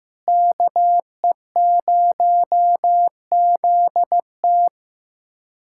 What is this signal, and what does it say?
Morse code, heard as a keyed tone, about 15 wpm, 700 Hz. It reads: KE0ZT
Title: USA Callsign DE Callsign - 5 Characters - 15wpm